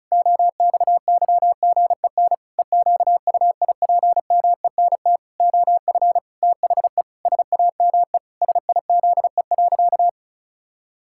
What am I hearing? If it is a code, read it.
Morse code, heard as a keyed tone, about 34 wpm, 700 Hz. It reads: OXYGEN EQUIPMENT OF THE SAME SIZE.
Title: Pangram Challenge v2 - 35wpm